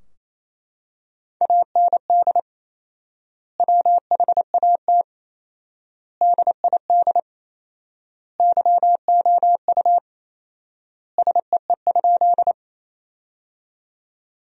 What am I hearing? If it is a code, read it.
Morse code, heard as a keyed tone, about 28 wpm, 700 Hz. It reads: AND WHAT DID YOU SEE?
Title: 4x Word-Spacing - Sentences from Top 100 Words - 28wpm